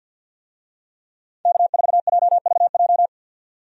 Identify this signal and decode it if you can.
Morse code, heard as a keyed tone, about 49 wpm, 700 Hz. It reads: K4JVJ
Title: USA Call Signs - 5 Characters - 50wpm